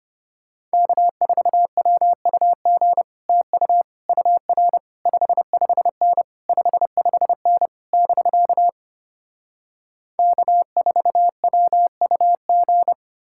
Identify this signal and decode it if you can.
Morse code, heard as a keyed tone, about 33 wpm, 700 Hz. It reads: K4WUG TU UR 55N 55N <BK> K4WUG